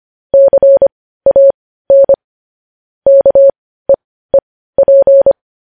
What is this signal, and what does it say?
Morse code, heard as a keyed tone, about 25 wpm, spaced as for 15 wpm, 550 Hz. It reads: CAN KEEP
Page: news headlines in morse code at 15 wpm